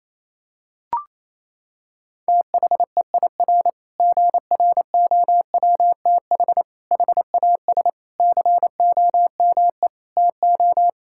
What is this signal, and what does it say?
Morse code, heard as a keyed tone, about 28 wpm, 700 Hz. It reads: THEIR GROWTH HAS COME TO
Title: Sentences from Top 1000 Words - Review Part 5 - 28wpm